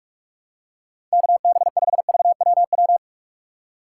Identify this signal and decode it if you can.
Morse code, heard as a keyed tone, about 43 wpm, 700 Hz. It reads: KB5VWW
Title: USA Call Signs - 6x Character Spacing - 45wpm